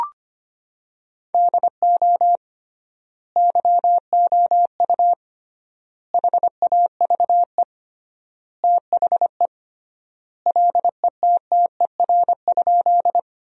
Morse code, heard as a keyed tone, about 25 wpm, 700 Hz. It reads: DO YOU HAVE THE LETTER?
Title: 3x Word-Spacing - Sentences from Top 300 Words - 25wpm